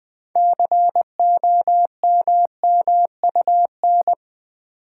Morse code, heard as a keyed tone, about 20 wpm, 700 Hz. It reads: COMMUN
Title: Build A Word - 3 Letters At A Time - 9-Letter Words - 20wpm